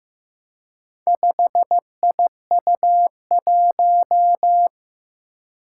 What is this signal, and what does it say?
Morse code, heard as a keyed tone, about 15 wpm, 700 Hz. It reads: 5IU1
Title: Four Challenging Letter-Number Combinations 15wpm